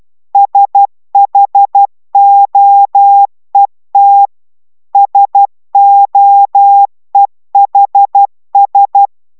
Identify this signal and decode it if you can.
Morse code, heard as a keyed tone, about 12 wpm, 800 Hz. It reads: SHOET SOEHS